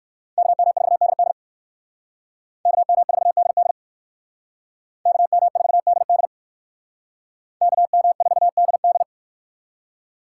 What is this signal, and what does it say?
Morse code, heard as a keyed tone, about 57 wpm, 700 Hz. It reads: KM4DD KM4DD KM4DD KM4DD